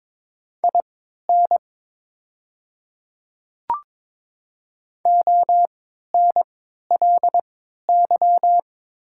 Morse code, heard as a keyed tone, about 22 wpm, spaced as for 13 wpm, 700 Hz. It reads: IN ONLY
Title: 3x Character Spacing - Top 200 Words - 22wpm